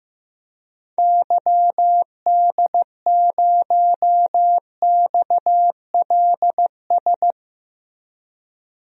Morse code, heard as a keyed tone, about 15 wpm, 700 Hz. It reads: YD0XLS
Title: DXCC-100 Call Signs - 6 Characters - 15wpm